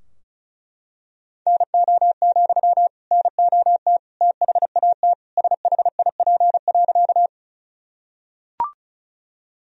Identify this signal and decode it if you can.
Morse code, heard as a keyed tone, about 36 wpm, 700 Hz. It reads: NO, NOT THAT SHIP.
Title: Sentences from Top 1000 Words - Review Part 2 - 35wpm